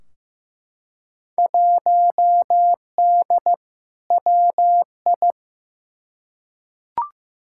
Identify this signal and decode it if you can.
Morse code, heard as a keyed tone, about 15 wpm, 700 Hz. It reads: 1D WI
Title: Field Day Exchanges - 15wpm